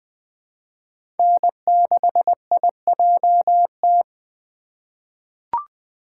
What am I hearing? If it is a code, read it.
Morse code, heard as a keyed tone, about 20 wpm, 700 Hz. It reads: N6IJT
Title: USA Call Signs - 5 Characters - 4x Character Spacing - 20wpm